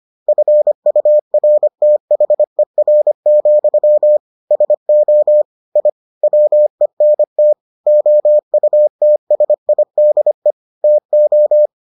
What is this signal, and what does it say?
Morse code, heard as a keyed tone, about 25 wpm, 600 Hz. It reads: FURTHER, SO I WENT OUTSIDE TO